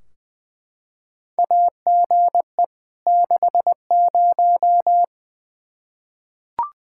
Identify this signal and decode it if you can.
Morse code, heard as a keyed tone, about 20 wpm, 700 Hz. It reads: AGE 60
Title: Practice QSO - Exchange 3 - More Info - 20wpm